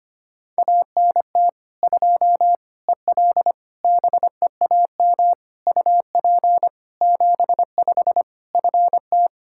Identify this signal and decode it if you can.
Morse code, heard as a keyed tone, about 25 wpm, 700 Hz. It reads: ANT 2 EL BEAM UP 75 FT
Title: Practice QSO - Exchange 2 - Info - 25wpm